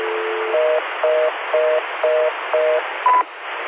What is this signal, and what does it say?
Reorder tone (North America), 480 + 620 Hz, with noise.